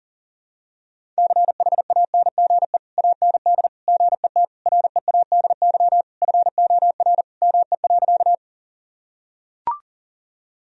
Morse code, heard as a keyed tone, about 37 wpm, 700 Hz. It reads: CHANGE AND GET READY FOR ME.
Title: Sentences from Top 1000 Words - Review Part 1 - 40wpm